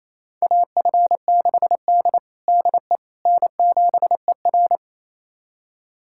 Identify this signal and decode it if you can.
Morse code, heard as a keyed tone, about 28 wpm, 700 Hz. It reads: AF6D DE N7ER
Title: USA Callsign DE Callsign - 4 Characters - 28wpm